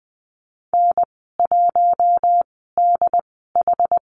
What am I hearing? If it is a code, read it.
Morse code, heard as a keyed tone, about 20 wpm, 700 Hz. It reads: N 1 D H
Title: 082 Lesson 14g - Number 2 - Call Signs 20wpm